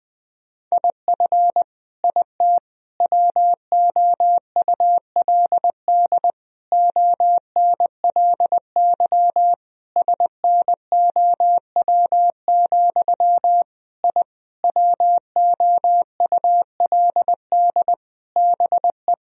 Morse code, heard as a keyed tone, about 21 wpm, 700 Hz. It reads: IF IT WOULD ONLY SNOW, I WOULD BE
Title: Sentences from Top 1000 Words - Review Part 3 - 20wpm